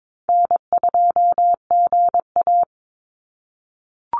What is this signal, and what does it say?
Morse code, heard as a keyed tone, about 22 wpm, 700 Hz. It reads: N2GA
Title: Call Signs - 22wpm